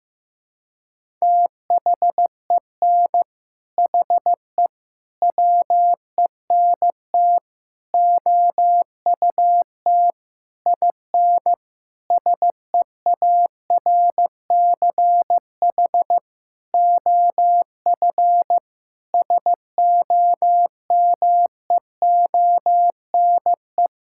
Morse code, heard as a keyed tone, about 16 wpm, 700 Hz. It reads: THEN HE WENT OUT IN SEARCH OF SOMEONE
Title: Sentences from Top 1000 Words - Review Part 5 - 15wpm